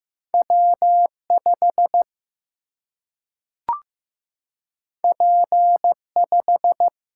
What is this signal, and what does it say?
Morse code, heard as a keyed tone, about 15 wpm, 700 Hz. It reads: W 5 P 5
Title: Single-Letter Single-Number 15wpm